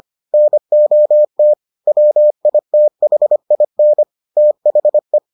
Morse code, heard as a keyed tone, about 25 wpm, 600 Hz. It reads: NOT WITHIN THE